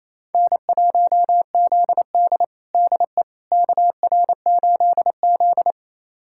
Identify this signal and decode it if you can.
Morse code, heard as a keyed tone, about 28 wpm, 700 Hz. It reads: N1ZD DE KR8Z
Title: USA Callsign DE Callsign - 4 Characters - 28wpm